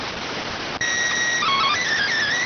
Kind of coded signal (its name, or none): SSTV (slow-scan television)